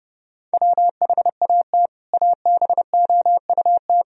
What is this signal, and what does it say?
Morse code, heard as a keyed tone, about 32 wpm, 700 Hz. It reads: WHAT ABOUT
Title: Sentences from Top 1000 Words - Review Part 1 - 30wpm